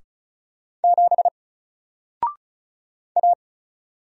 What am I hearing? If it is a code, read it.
Morse code, heard as a keyed tone, about 35 wpm, 700 Hz. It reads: 7 A
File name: Single-Letter-Number-35wpm